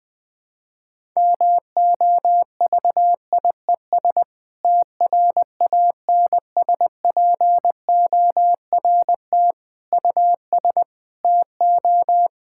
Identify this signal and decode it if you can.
Morse code, heard as a keyed tone, about 19 wpm, 700 Hz. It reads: MOVIES TRANSPORT US TO
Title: Difficult Sentences with Hints - 20wpm